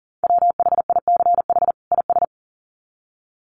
Morse code, heard as a keyed tone, about 40 wpm, 700 Hz. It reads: WHICH IS
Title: Sets of 2 Words 40wpm